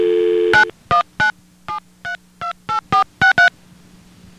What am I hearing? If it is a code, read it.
DTMF, heard as telephone touch tones, tones of about 120 ms. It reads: 9197328463